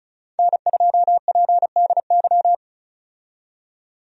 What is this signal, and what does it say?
Morse code, heard as a keyed tone, about 34 wpm, 700 Hz. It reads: N2PDY
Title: USA Call Signs - 5 Characters - 35wpm